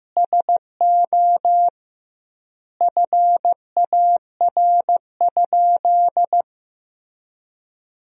Morse code, heard as a keyed tone, about 15 wpm, 700 Hz. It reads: SO FAR?
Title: 2x Word-Spacing - Sentences from Top 300 Words - 15wpm